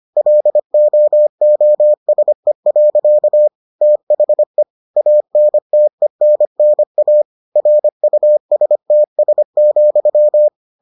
Morse code, heard as a keyed tone, about 25 wpm, 600 Hz. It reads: LOOSE. THE ANTENNA RUSTS,